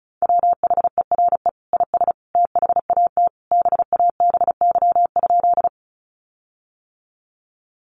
Morse code, heard as a keyed tone, about 34 wpm, 700 Hz. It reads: WHERE IS THAT BABY?
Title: Sentences from Top 1300 Words 35wpm